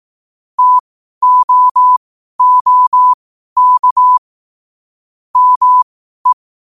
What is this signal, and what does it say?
Morse code, heard as a keyed tone, about 18 wpm, spaced as for 12 wpm, 1000 Hz. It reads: TOOK ME
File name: qotdfiles / 12wpm